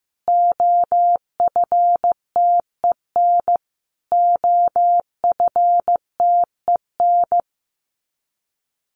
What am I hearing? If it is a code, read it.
Morse code, heard as a keyed tone, about 14 wpm, 700 Hz. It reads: OFTEN OFTEN
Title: Top 500 Words Repeated 15wpm